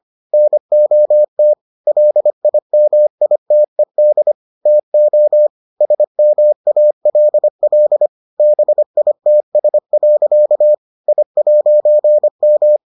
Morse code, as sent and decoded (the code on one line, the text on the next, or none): -. --- - / .-.. .. -- .. - . -.. / - --- / ... -- .- .-.. .-.. / -... .. - ... .-.-.- / .. .----. --
NOT LIMITED TO SMALL BITS. I'M